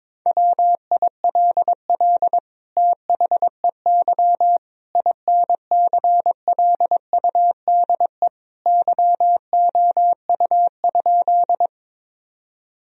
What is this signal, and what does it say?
Morse code, heard as a keyed tone, about 22 wpm, 700 Hz. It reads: WILL THEY INCLUDE YOU?
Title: Sentences from Top 1000 Words - Review Part 3 - 22wpm